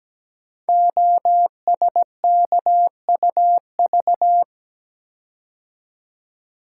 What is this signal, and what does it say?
Morse code, heard as a keyed tone, about 17 wpm, 700 Hz. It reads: OSKUV
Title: Code Groups - 5 Letters - 17wpm